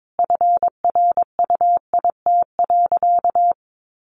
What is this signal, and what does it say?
Morse code, heard as a keyed tone, about 22 wpm, 700 Hz. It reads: FRUIT.
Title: Sentences from Top 1300 Words 22wpm